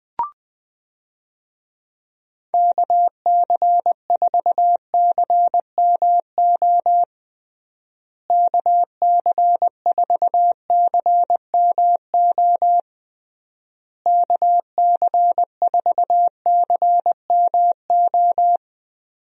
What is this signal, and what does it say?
Morse code, heard as a keyed tone, about 20 wpm, 700 Hz. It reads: KC4CMO KC4CMO KC4CMO
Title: USA Call Signs - 6 Characters - Repeated 3x - 20wpm